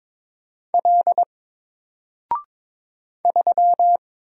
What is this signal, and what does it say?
Morse code, heard as a keyed tone, about 22 wpm, 700 Hz. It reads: L 3
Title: Single Character (Letter:Number:Punctuation:Prosign) Alt 22wpm